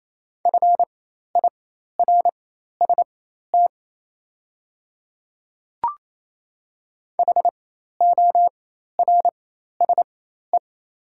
Morse code, heard as a keyed tone, about 27 wpm, spaced as for 13 wpm, 690 Hz. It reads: FIRST HORSE
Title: 4x Character Spacing - Top 500 Words - 28wpm